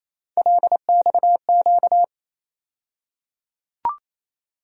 Morse code, heard as a keyed tone, about 28 wpm, 700 Hz. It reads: LXQ
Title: USA Call Sign Postfixes - 3 Letters 28wpm